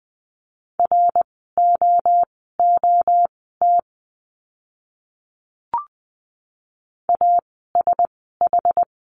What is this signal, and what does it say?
Morse code, heard as a keyed tone, about 20 wpm, spaced as for 14 wpm, 700 Hz. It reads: ROOT ASH
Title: 061 Lesson 11f - Letter h - Words 20wpm